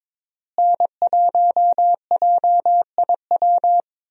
Morse code, heard as a keyed tone, about 22 wpm, 700 Hz. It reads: N1JIW
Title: USA Call Signs - 5 Characters - Repeated 3x - 22wpm